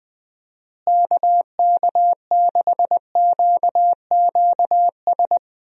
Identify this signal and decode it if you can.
Morse code, heard as a keyed tone, about 20 wpm, 700 Hz. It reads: KK6QQS
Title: USA Call Signs - 6 Characters - Repeated 2x - 20wpm